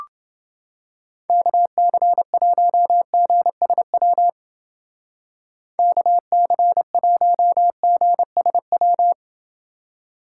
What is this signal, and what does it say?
Morse code, heard as a keyed tone, about 28 wpm, 700 Hz. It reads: KC1GSW KC1GSW